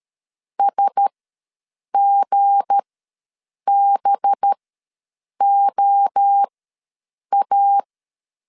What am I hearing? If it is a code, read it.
Morse code, heard as a keyed tone, about 13 wpm, spaced as for 7 wpm, 800 Hz. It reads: SGBOA
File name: LEZIONI / 7 GF9Z / 7 G ws 12wpm 3140